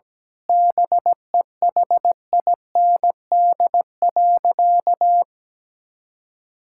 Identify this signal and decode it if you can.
Morse code, heard as a keyed tone, about 17 wpm, 700 Hz. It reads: BEHIND.
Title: Sentences from Top 1600 Words 17wpm